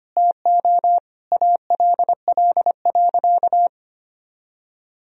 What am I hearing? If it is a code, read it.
Morse code, heard as a keyed tone, about 25 wpm, 700 Hz. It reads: TO ALL.
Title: Sentences from Top 1000 Words - Review Part 3 - 25wpm